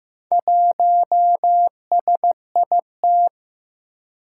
Morse code, heard as a keyed tone, about 15 wpm, 700 Hz. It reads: 1SIT
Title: Four Challenging Letter-Number Combinations 15wpm